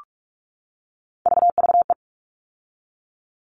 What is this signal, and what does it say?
Morse code, heard as a keyed tone, about 44 wpm, 700 Hz. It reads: VVE
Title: All Cut Numbers - 3-Digit 45wpm